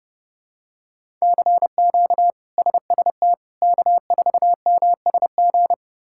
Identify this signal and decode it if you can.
Morse code, heard as a keyed tone, about 30 wpm, 700 Hz. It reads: CQ SST K4MSG
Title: K1USN SST QSO Practice - 30wpm